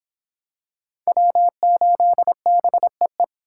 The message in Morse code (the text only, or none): W8BEE